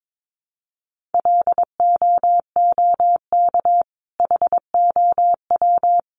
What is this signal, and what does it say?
Morse code, heard as a keyed tone, about 22 wpm, 700 Hz. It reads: LOOK HOW
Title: Sentences from Top 1400 Words 22wpm